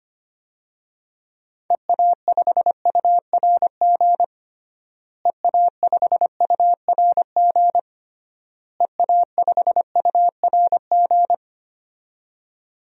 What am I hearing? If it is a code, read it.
Morse code, heard as a keyed tone, about 25 wpm, 700 Hz. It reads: EA5URG EA5URG EA5URG
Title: DXCC-100 Call Signs - 6 Characters - Repeated 3x - 25wpm